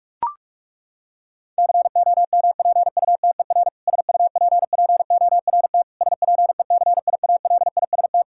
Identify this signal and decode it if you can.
Morse code, heard as a keyed tone, about 41 wpm, 700 Hz. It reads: COMPUTER SUPPORT SPECIALIST
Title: Practice QSO - Exchange 3 - More Info - 45wpm